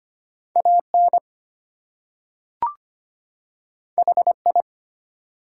Morse code, heard as a keyed tone, about 25 wpm, 700 Hz. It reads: AN HI
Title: Two Letter Words - 25wpm